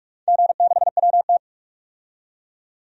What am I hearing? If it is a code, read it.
Morse code, heard as a keyed tone, about 42 wpm, 700 Hz. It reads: G6WT
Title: DXCC-100 Call Signs - 4 Characters - 45wpm